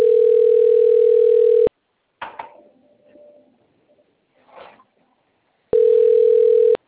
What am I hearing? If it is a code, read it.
Ringback tone (North America), 440 + 480 Hz.
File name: rg-1600-unknown- - - .89681